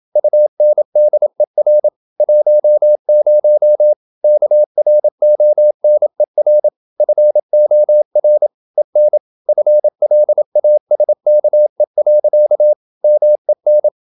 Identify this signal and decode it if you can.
Morse code, heard as a keyed tone, about 27 wpm, 600 Hz. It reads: UNDER 10 KRONER FOR EN FLASKE. MEN
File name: Ferie QRQ 27wpm